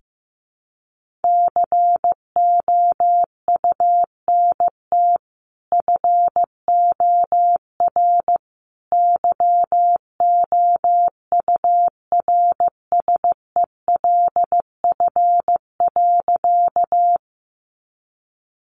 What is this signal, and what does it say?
Morse code, heard as a keyed tone, about 15 wpm, 700 Hz. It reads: COUNT FOR YOURSELF.
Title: Sentences from Top 1500 Words 15wpm